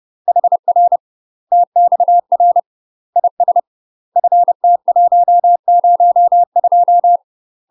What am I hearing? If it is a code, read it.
Morse code, heard as a keyed tone, about 30 wpm, 700 Hz. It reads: HR TXR IS FT102
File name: UK MASTER QSOs TRAINING / 30WPM / UK MASTER QSOs 0039